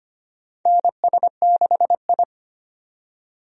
Morse code, heard as a keyed tone, about 25 wpm, 700 Hz. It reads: NS6I
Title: USA Call Signs - 4 Characters - 25wpm